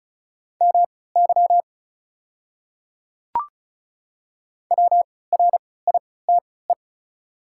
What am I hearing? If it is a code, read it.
Morse code, heard as a keyed tone, about 36 wpm, spaced as for 20 wpm, 700 Hz. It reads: MY WRITE
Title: 3x Character Spacing - Top 100 Words - 35wpm